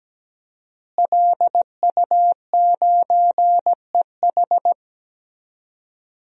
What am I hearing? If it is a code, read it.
Morse code, heard as a keyed tone, about 17 wpm, 700 Hz. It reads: LU9EH
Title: DXCC-100 Call Signs - 5 Characters - Repeated 3x - 17wpm